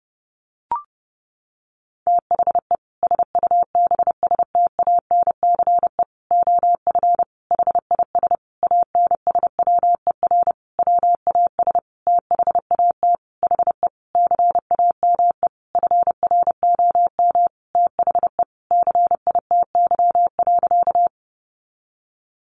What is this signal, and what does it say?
Morse code, heard as a keyed tone, about 30 wpm, 700 Hz. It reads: THE SUBSTANCE OF HIS ANSWER WAS THAT HE CAME FROM THE CITY.
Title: Sentences from Top 1500 Words 30wpm